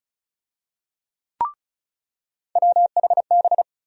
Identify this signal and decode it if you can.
Morse code, heard as a keyed tone, about 36 wpm, 700 Hz. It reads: WHB
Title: USA Call Sign Postfixes - 3 Letters 35wpm